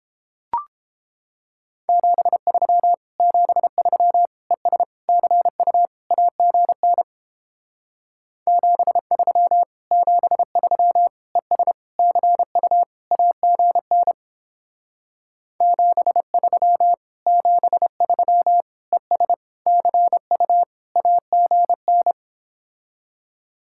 Morse code, heard as a keyed tone, about 29 wpm, 700 Hz. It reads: 73 73 ES CU AGN 73 73 ES CU AGN 73 73 ES CU AGN